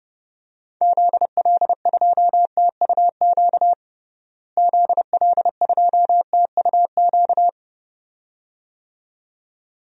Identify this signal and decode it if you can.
Morse code, heard as a keyed tone, about 30 wpm, 700 Hz. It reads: ZL2TUQ ZL2TUQ
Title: DXCC-100 Call Signs - 6 Characters - Repeated 2x - 30wpm